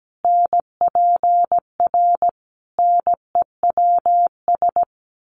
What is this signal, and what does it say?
Morse code, heard as a keyed tone, about 17 wpm, 700 Hz. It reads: NPR NEWS
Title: Sets of 5 Words 17wpm